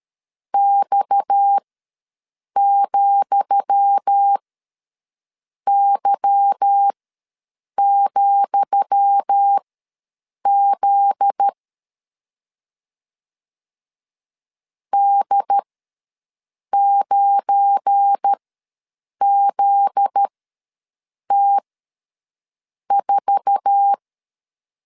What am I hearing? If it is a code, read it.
Morse code, heard as a keyed tone, about 13 wpm, spaced as for 6 wpm, 800 Hz. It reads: X,Y,Z D9ZT4
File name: LEZIONI / 10 SIGNS / 10D COMA ws 12wpm 3618